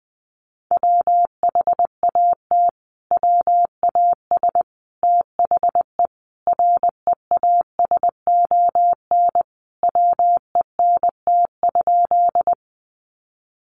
Morse code, heard as a keyed tone, about 20 wpm, 700 Hz. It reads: WHAT WAS THE REASON WENT?
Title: Sentences from Top 700 Words 20wpm